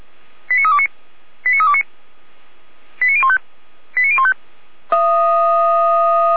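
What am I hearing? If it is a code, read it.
Five-tone selective call (ZVEI), heard as quick runs of five tones, tones of about 70 ms. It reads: 89329 89329 89015 89015
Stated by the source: ZVEI five-tone selective call received on an SDR, archived by fire brigade Orlamünde (feuerwehr-orlamuende).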